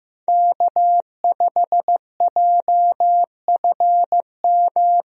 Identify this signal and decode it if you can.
Morse code, heard as a keyed tone, about 16 wpm, 700 Hz. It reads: K5JFM
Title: USA Call Signs - 5 Characters - Repeated 3x - 15wpm